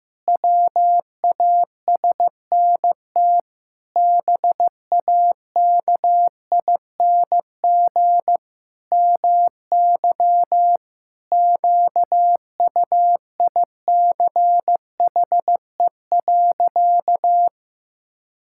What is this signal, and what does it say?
Morse code, heard as a keyed tone, about 15 wpm, 700 Hz. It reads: WASNT BAKING MY QUICHE.
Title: Pangram Challenge v2 - 15wpm